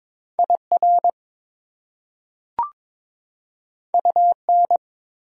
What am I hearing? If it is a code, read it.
Morse code, heard as a keyed tone, about 22 wpm, 700 Hz. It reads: IR UN
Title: Prefixes and Suffixes 22wpm